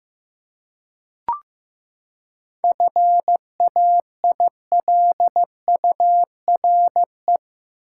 Morse code, heard as a keyed tone, about 15 wpm, 700 Hz. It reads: FAILURE
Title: Seven Letter Words - 15wpm